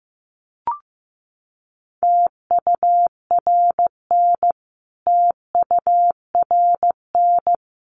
Morse code, heard as a keyed tone, about 15 wpm, 700 Hz. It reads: TURN TURN
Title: Top 200 Words Repeated 15wpm